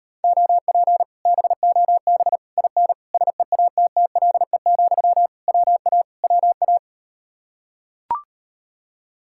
Morse code, heard as a keyed tone, about 38 wpm, 700 Hz. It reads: OP BOB IN SEATTLE, WA WA